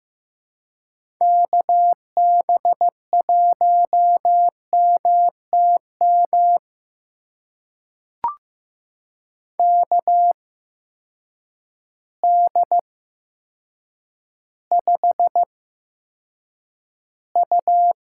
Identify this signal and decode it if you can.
Morse code, heard as a keyed tone, about 15 wpm, spaced as for 7 wpm, 700 Hz. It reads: KB1MTM KD5U
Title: USA Call Signs - 6 Characters - 8x Character Spacing - 15wpm